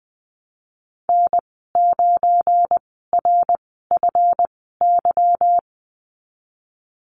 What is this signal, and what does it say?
Morse code, heard as a keyed tone, about 20 wpm, spaced as for 15 wpm, 700 Hz. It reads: N9RFY
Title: 186 Lesson 30g - Letter b - Call Signs 20wpm